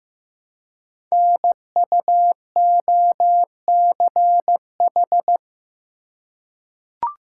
Morse code, heard as a keyed tone, about 15 wpm, 700 Hz. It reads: NUOCH
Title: Code Groups - 5 Letters - 15wpm